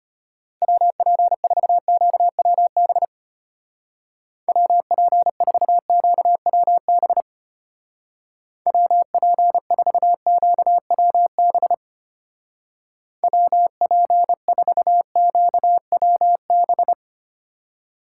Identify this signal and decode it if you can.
Morse code, heard as a keyed tone, about 32 wpm, 700 Hz. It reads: WP4QWB WP4QWB WP4QWB WP4QWB